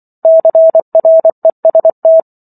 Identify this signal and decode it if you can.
Morse code, heard as a keyed tone, about 24 wpm, 650 Hz. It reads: CREST
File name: Chapter 15 - 24 WPM